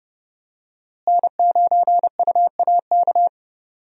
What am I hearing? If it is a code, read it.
Morse code, heard as a keyed tone, about 30 wpm, 700 Hz. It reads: N9UAK
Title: USA Call Signs - 5 Characters - Repeated 2x - 30wpm